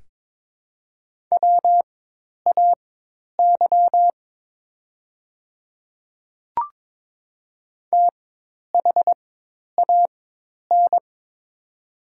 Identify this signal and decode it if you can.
Morse code, heard as a keyed tone, about 22 wpm, spaced as for 10 wpm, 700 Hz. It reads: WAY THAN
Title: 4x Character Spacing - Top 100 Words - 22wpm